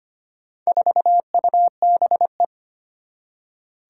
Morse code, heard as a keyed tone, about 25 wpm, 700 Hz. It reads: 4UBE
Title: Four Challenging Letter-Number Combinations 25wpm